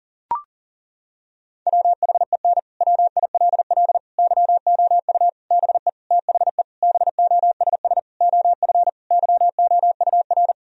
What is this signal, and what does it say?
Morse code, heard as a keyed tone, about 39 wpm, 700 Hz. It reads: WHEN WILL YOU BE THE BOSS OF YOUR
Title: Sentences from Top 1000 Words - Review Part 4 - 40wpm